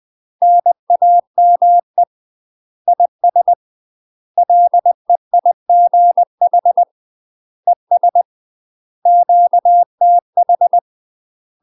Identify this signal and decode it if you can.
Morse code, heard as a keyed tone, about 20 wpm, 700 Hz. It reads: NAME IS LEIGH ES QTH
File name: UK MASTER QSOs TRAINING / 20WPM / UK MASTER QSOs 0051